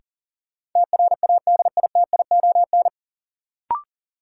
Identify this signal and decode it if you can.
Morse code, heard as a keyed tone, about 40 wpm, 700 Hz. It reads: TRADITION
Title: Sentences from Top 1500 Words - New Word Drill - 40wpm